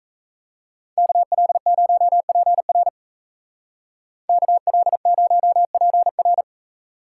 Morse code, heard as a keyed tone, about 33 wpm, 700 Hz. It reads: KL0PR KL0PR
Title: Speed Racing - USA 2x2 Call Signs 28wpm